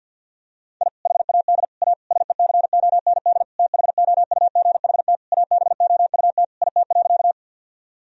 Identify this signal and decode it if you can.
Morse code, heard as a keyed tone, about 52 wpm, 700 Hz. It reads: I HAD A SECOND THOUGHT ABOUT IT.
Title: Sentences from Top 1000 Words - Review Part 1 - 50wpm